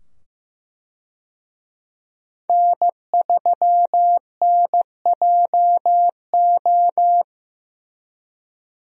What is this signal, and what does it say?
Morse code, heard as a keyed tone, about 15 wpm, 700 Hz. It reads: N3NJO
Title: USA Call Signs - 5 Characters - 15wpm